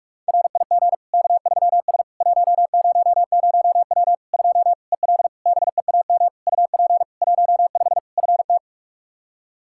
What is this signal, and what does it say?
Morse code, heard as a keyed tone, about 46 wpm, 700 Hz. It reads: RIG K3S 100W 2 EL BEAM UP 15 FT
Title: Practice QSO - Exchange 2 - Info - 45wpm